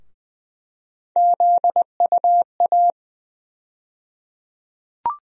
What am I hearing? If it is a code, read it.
Morse code, heard as a keyed tone, about 20 wpm, 700 Hz. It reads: ZUA
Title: USA Call Sign Postfixes - 3 Letters 20wpm